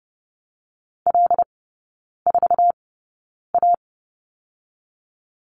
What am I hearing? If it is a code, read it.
Morse code, heard as a keyed tone, about 30 wpm, 700 Hz. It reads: L 4 A
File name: 081 Lesson 14e - Number 2 - Three Characters 30wpm